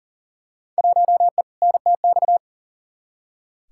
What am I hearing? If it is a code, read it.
Morse code, heard as a keyed tone, about 40 wpm, 700 Hz. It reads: 1E NTX
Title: Field Day Exchanges - 40wpm